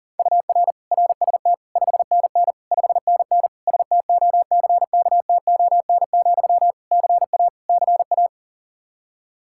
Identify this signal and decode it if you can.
Morse code, heard as a keyed tone, about 40 wpm, 700 Hz. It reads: UR RST 5NN 5NN STOCKTON, CA CA
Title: Practice QSO - Exchange 1 - Beginning - 40wpm